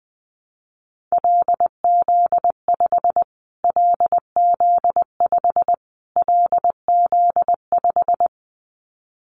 Morse code, heard as a keyed tone, about 20 wpm, 700 Hz. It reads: LZ5 LZ5 LZ5
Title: Callsign Prefixes for Top 25 DXCC - Repeated 3x - 20wpm